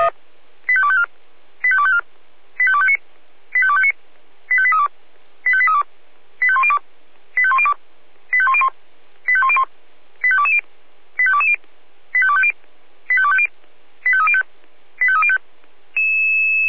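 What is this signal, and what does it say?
Five-tone selective call (ZVEI), heard as quick runs of five tones, tones of about 70 ms. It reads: 85254 85254 85259 85259 85832 85832 85102 85102 85101 85101 85220 85220 85250 85250 85305 85305
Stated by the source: ZVEI five-tone call received by SDR.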